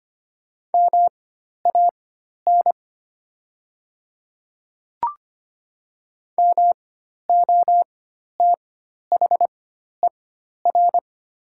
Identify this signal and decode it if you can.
Morse code, heard as a keyed tone, about 25 wpm, spaced as for 12 wpm, 700 Hz. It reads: MAN MOTHER
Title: 4x Character Spacing - Top 200 Words - 25wpm